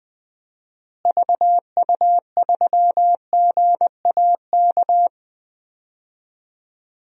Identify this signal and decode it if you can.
Morse code, heard as a keyed tone, about 20 wpm, 700 Hz. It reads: VU3GAK
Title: DXCC-100 Call Signs - 6 Characters - Repeated 2x - 20wpm